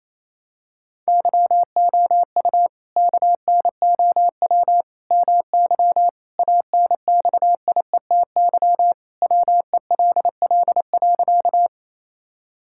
Morse code, heard as a keyed tone, about 28 wpm, 700 Hz. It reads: YOU KNOW MY ANXIETY WELL.
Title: Sentences from Top 1600 Words 28wpm